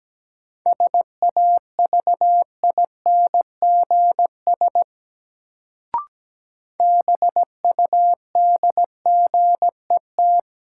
Morse code, heard as a keyed tone, about 17 wpm, 700 Hz. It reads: SAVINGS BUDGET
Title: Sentences from Top 1200 Words - New Word Drill - 17wpm